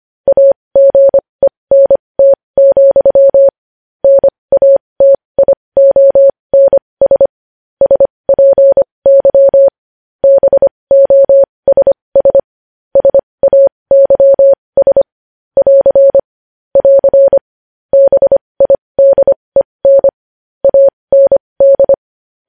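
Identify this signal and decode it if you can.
Morse code, heard as a keyed tone, about 25 wpm, spaced as for 20 wpm, 550 Hz. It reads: AGENT, NATIONS SPY BOSS SAYS <AR> <AR> BIDEN AND
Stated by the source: Morse code practice sending news headlines at 20 wpm.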